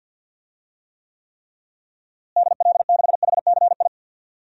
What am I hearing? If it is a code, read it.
Morse code, heard as a keyed tone, about 52 wpm, 700 Hz. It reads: DL6HCI
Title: DXCC-100 Call Signs - 6 Characters - Repeated 3x - 50wpm